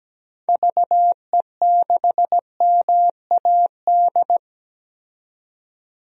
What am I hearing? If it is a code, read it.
Morse code, heard as a keyed tone, about 17 wpm, 700 Hz. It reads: VE6MAD
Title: DXCC-100 Call Signs - 6 Characters - 17wpm